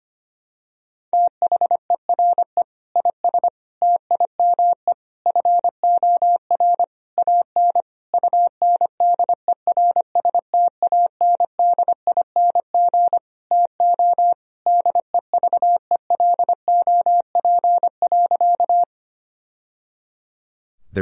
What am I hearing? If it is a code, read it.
Morse code, heard as a keyed tone, about 25 wpm, 700 Hz. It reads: THERE IS TIME FOR AN UNDERSTANDING TO DEVELOP.
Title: Sentences from Top 1000 Words - Review Part 3 - 25wpm